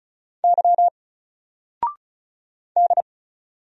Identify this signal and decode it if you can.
Morse code, heard as a keyed tone, about 36 wpm, 700 Hz. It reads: Y D
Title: Single Letter-Number Alt 35wpm